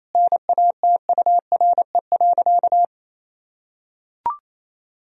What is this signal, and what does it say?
Morse code, heard as a keyed tone, about 28 wpm, 700 Hz. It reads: NATURE.
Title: Sentences from Top 1000 Words - Review Part 4 - 28wpm